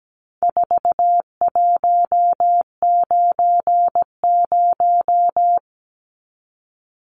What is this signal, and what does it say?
Morse code, heard as a keyed tone, about 17 wpm, 700 Hz. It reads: 4190
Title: Numbers - 4-Digit 17wpm